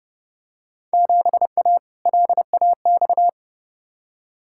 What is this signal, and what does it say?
Morse code, heard as a keyed tone, about 30 wpm, 700 Hz. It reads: 7A LAX
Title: Field Day Exchanges Encore - 30wpm